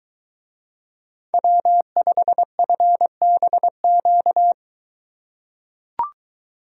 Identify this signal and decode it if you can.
Morse code, heard as a keyed tone, about 21 wpm, 700 Hz. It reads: W5FBQ